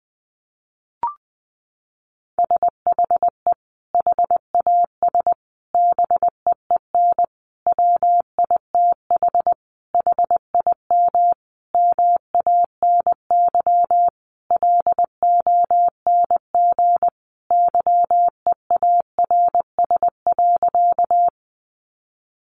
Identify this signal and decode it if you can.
Morse code, heard as a keyed tone, about 20 wpm, 700 Hz. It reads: SHE HAS BEEN WITH HIM MANY LONG YEARS.
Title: Sentences from Top 200 Words 20wpm